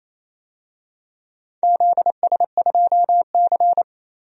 Morse code, heard as a keyed tone, about 28 wpm, 700 Hz. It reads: ZS2C
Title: DXCC-100 Call Signs - 4 Characters - 28wpm